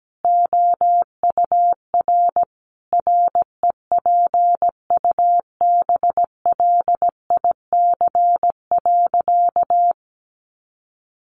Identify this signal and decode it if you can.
Morse code, heard as a keyed tone, about 17 wpm, 700 Hz. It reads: OUR REPUBLIC.
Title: Sentences from Top 1500 Words 17wpm